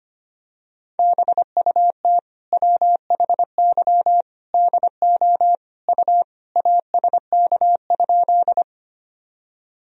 Morse code, heard as a keyed tone, about 25 wpm, 700 Hz. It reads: BUT WHY DO U ASK?
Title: Sentences from Top 200 Words and Abbreviations 25wpm